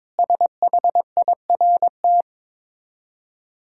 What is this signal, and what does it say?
Morse code, heard as a keyed tone, about 23 wpm, 690 Hz. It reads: SHIRT
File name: Difficult Words with Hints - 22wpm